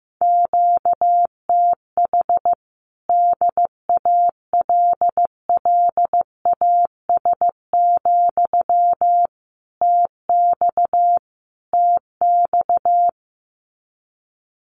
Morse code, heard as a keyed tone, about 15 wpm, 700 Hz. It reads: QTH DALLAS, TX TX
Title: Common QSO Phrases Encore 15wpm